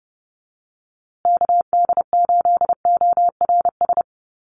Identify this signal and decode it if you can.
Morse code, heard as a keyed tone, about 30 wpm, 700 Hz. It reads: KD8ORS
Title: USA Call Signs Repeated 2x 30wpm